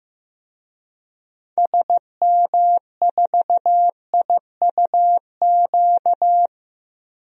Morse code, heard as a keyed tone, about 16 wpm, 700 Hz. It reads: SM4IUQ
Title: DXCC-100 Call Signs - 6 Characters - 15wpm